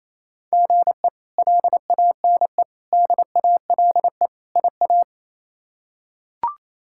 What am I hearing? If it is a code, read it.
Morse code, heard as a keyed tone, about 28 wpm, 700 Hz. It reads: GE LANE DALE IA
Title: K1USN SST QSO Practice - 28wpm